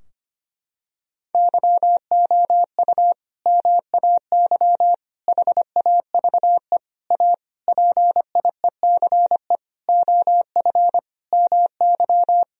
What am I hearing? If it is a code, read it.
Morse code, heard as a keyed tone, about 25 wpm, 700 Hz. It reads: YOU MAY HAVE A PIECE OF MY
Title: Sentences from Top 1000 Words - Review Part 2 - 25wpm